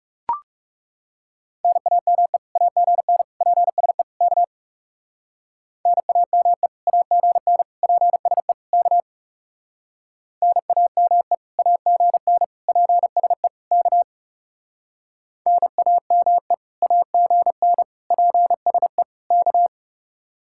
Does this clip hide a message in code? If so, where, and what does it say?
Yes: Morse code from 1.6 to 19.7 s, heard as a keyed tone, about 38 wpm, 700 Hz. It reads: NAME AGN PSE K NAME AGN PSE K NAME AGN PSE K NAME AGN PSE K